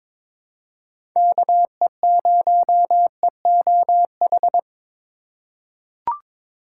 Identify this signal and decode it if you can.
Morse code, heard as a keyed tone, about 23 wpm, 700 Hz. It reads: KE0EOH
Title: USA Call Signs - 4x Character Spacing - 22wpm